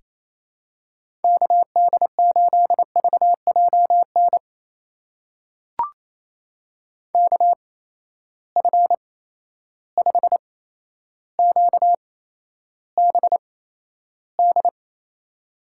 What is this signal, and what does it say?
Morse code, heard as a keyed tone, about 28 wpm, spaced as for 12 wpm, 700 Hz. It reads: KD8VJN KF5QBD
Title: USA Call Signs - 6 Characters - 8x Character Spacing - 28wpm